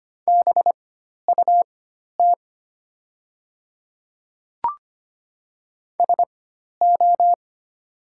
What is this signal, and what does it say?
Morse code, heard as a keyed tone, about 25 wpm, spaced as for 12 wpm, 700 Hz. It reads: BUT SO
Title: 4x Character Spacing - Top 100 Words - 25wpm